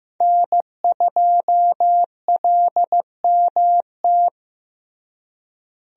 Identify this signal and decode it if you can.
Morse code, heard as a keyed tone, about 15 wpm, 700 Hz. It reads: N2LMT
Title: USA Call Signs - 5 Characters - Repeated 3x - 15wpm